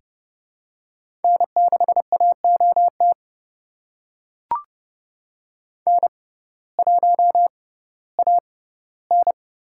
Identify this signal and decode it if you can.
Morse code, heard as a keyed tone, about 29 wpm, spaced as for 16 wpm, 700 Hz. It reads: N6AOT N1AN
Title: USA Call Signs - 5 Characters - 6x Character Spacing - 30wpm